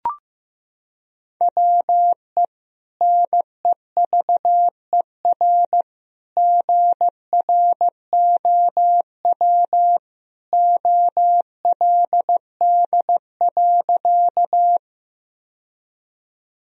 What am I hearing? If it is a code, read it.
Morse code, heard as a keyed tone, about 15 wpm, 700 Hz. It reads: WE NEVER GROW OLD.
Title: Sentences from Top 1000 Words - Review Part 1 - 15wpm